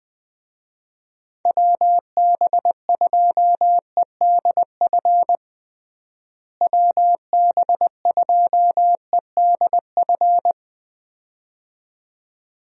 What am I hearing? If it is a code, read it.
Morse code, heard as a keyed tone, about 21 wpm, 700 Hz. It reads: WB2EDF WB2EDF
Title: USA Call Signs - 6 Characters - Repeated 2x - 20wpm